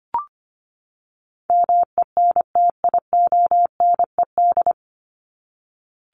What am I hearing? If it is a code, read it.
Morse code, heard as a keyed tone, about 25 wpm, 700 Hz. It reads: MENTIONED
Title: ED Suffix - Top 100 Words with Base Words 25wpm